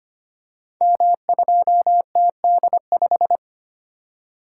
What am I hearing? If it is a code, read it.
Morse code, heard as a keyed tone, about 25 wpm, 700 Hz. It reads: M2TD5
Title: Code Groups - 5 Letter:Numbers - 25wpm